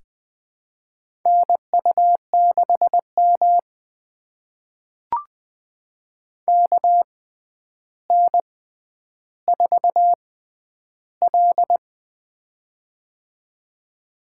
Morse code, heard as a keyed tone, about 20 wpm, spaced as for 10 wpm, 700 Hz. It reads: NU6M KN4L
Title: USA Call Signs - 4 Characters - 6x Character Spacing - 20wpm